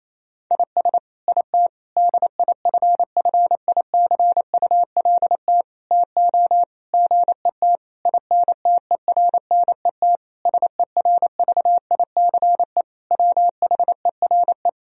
Morse code, heard as a keyed tone, about 28 wpm, 700 Hz. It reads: IS IT DIFFICULT TO GET INTERNET SERVICE WHERE
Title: Sentences from Top 1000 Words - Review Part 4 - 28wpm